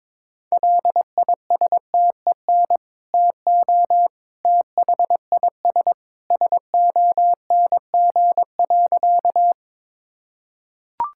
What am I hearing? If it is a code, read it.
Morse code, heard as a keyed tone, about 22 wpm, 700 Hz. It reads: LISTEN TO THIS SONG.
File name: practice / 22wpm / Sentences with 2-Word Phrases - 22wpm